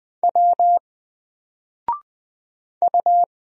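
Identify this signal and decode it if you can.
Morse code, heard as a keyed tone, about 20 wpm, 700 Hz. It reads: W U
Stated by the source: Single letter and number Morse drill at 20 wpm, from N4LCD.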